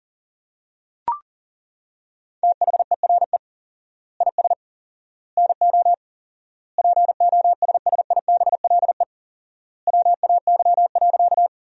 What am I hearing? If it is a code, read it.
Morse code, heard as a keyed tone, about 40 wpm, 700 Hz. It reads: THERE IS NO POSSIBLE WAY.
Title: 4x Word-Spacing - Sentences from Top 500 Words - 40wpm